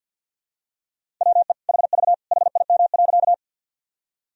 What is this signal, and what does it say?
Morse code, heard as a keyed tone, about 51 wpm, 700 Hz. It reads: WE HV HIM.
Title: Sentences from Top 100 Words and Abbreviations 50wpm